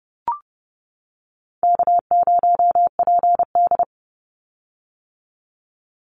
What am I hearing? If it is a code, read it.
Morse code, heard as a keyed tone, about 30 wpm, 700 Hz. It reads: K0PD
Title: Call Signs - 30wpm